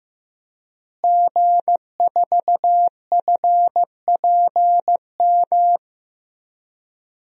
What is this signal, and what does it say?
Morse code, heard as a keyed tone, about 15 wpm, 700 Hz. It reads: G4FPM
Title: DXCC-100 Call Signs - 5 Characters - Repeated 2x - 15wpm